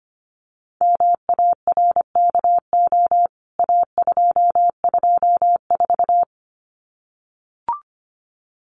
Morse code, heard as a keyed tone, about 25 wpm, 700 Hz. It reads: MARKO A224
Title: CWT Exchange 25wpm